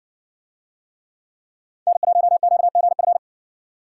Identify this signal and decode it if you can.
Morse code, heard as a keyed tone, about 60 wpm, 700 Hz. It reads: N1QGF